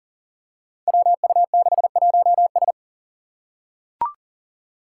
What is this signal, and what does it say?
Morse code, heard as a keyed tone, about 40 wpm, 700 Hz. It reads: WU61S
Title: Five Challenging Letter-Number Combinations 40wpm